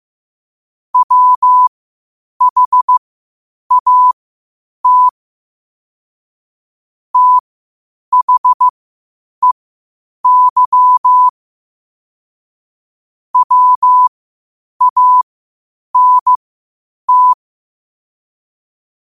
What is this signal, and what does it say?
Morse code, heard as a keyed tone, about 15 wpm, spaced as for 8 wpm, 1000 Hz. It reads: WHAT THEY WANT
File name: qotdfiles / 8wpm